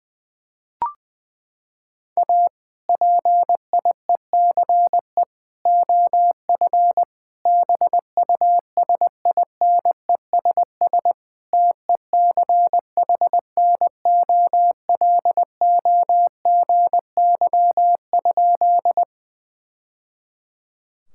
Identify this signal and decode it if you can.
Morse code, heard as a keyed tone, about 20 wpm, 700 Hz. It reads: A PIECE OF BUSINESS TECHNOLOGY?
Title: Sentences from Top 1000 Words - Review Part 5 - 20wpm